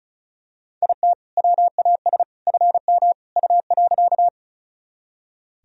Morse code, heard as a keyed tone, about 36 wpm, 700 Hz. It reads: IT WAS FM U.
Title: Sentences from Top 100 Words and Abbreviations 35wpm